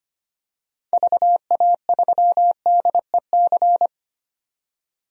Morse code, heard as a keyed tone, about 25 wpm, 700 Hz. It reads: VA3DEC
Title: DXCC-100 Call Signs - 6 Characters - 25wpm